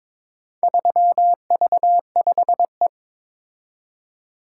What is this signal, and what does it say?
Morse code, heard as a keyed tone, about 22 wpm, 700 Hz. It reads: 3V5E
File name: practice / 22wpm / Four Letter-Number Combinations - 22wpm